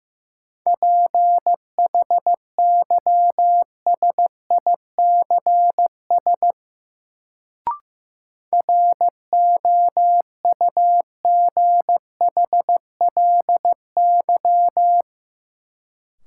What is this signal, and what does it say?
Morse code, heard as a keyed tone, about 16 wpm, 700 Hz. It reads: PHYSICS ROUGHLY
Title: Sentences from Top 1500 Words - New Word Drill Alt - 15wpm